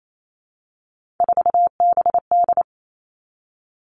Morse code, heard as a keyed tone, about 28 wpm, 700 Hz. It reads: 4BD
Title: Three Challenging Letter-Number Combinations 28wpm